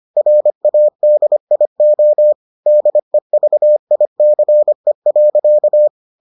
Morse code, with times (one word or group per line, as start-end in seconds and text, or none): 0.2-2.3 RADIO
2.7-5.9 DEVICE.